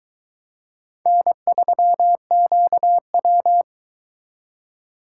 Morse code, heard as a keyed tone, about 22 wpm, 700 Hz. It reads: N3QW